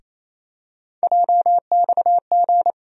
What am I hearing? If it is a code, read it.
Morse code, heard as a keyed tone, about 28 wpm, 700 Hz. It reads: JXG
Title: USA Call Sign Postfixes - 3 Letters 28wpm